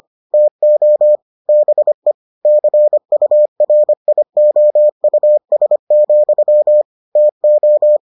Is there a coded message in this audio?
Yes: Morse code from 0.3 to 8.0 s, heard as a keyed tone, about 25 wpm, 600 Hz. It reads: TO BE CURIOUS, TO